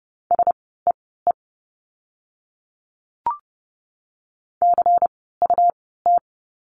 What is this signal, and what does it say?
Morse code, heard as a keyed tone, about 30 wpm, spaced as for 17 wpm, 700 Hz. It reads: SEE CUT
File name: Lesson 17f - Letter u - Words 30wpm